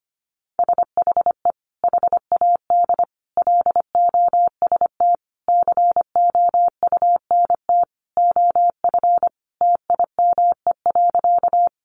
Morse code, heard as a keyed tone, about 25 wpm, 700 Hz. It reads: SHE HAD LOST COUNT OF TIME.
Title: Sentences from Top 1500 Words 25wpm